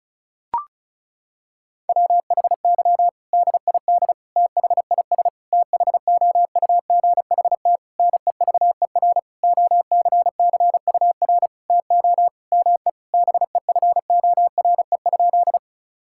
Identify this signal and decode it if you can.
Morse code, heard as a keyed tone, about 36 wpm, 700 Hz. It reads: WHY DID THIS THOUGHT NEVER OCCUR TO ME BEFORE?
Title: Sentences from Top 1000 Words - Review Part 6 - 35wpm